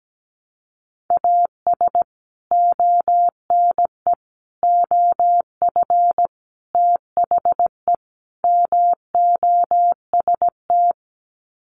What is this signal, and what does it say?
Morse code, heard as a keyed tone, about 17 wpm, 700 Hz. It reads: AS ONE OF THE MOST
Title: Sets of 5 Words 17wpm